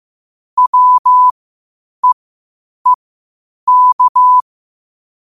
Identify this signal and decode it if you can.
Morse code, heard as a keyed tone, about 15 wpm, spaced as for 8 wpm, 1000 Hz. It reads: WEEK
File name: qotdfiles / 8wpm